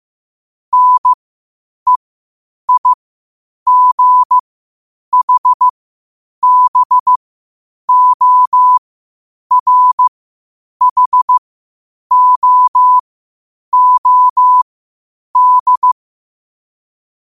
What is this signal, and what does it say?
Morse code, heard as a keyed tone, about 15 wpm, spaced as for 8 wpm, 1000 Hz. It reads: NEIGHBORHOOD
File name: qotdfiles / 8wpm